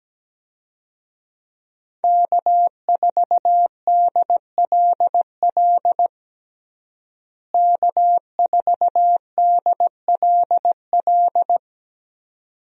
Morse code, heard as a keyed tone, about 17 wpm, 700 Hz. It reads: K4DLL K4DLL
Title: USA Call Signs - 5 Characters - Repeated 2x - 17wpm